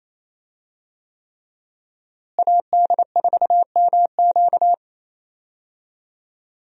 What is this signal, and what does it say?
Morse code, heard as a keyed tone, about 28 wpm, 700 Hz. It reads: AD4MQ
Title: USA Call Signs - 5 Characters - Repeated 3x - 28wpm